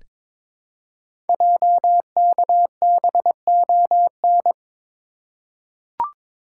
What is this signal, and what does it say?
Morse code, heard as a keyed tone, about 23 wpm, 700 Hz. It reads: JKBON
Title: Code Groups - 5 Letter:Numbers - 22wpm